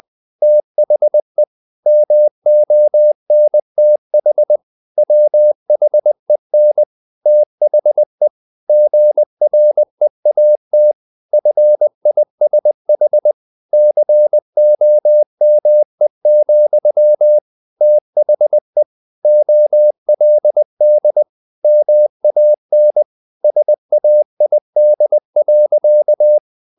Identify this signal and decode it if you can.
Morse code, heard as a keyed tone, about 20 wpm, 600 Hz. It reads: THE MONTH WHEN THE GREAT FISH COME, THE OLD MAN SAID.
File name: the old man and the sea / the old man and the sea part 1 20wpm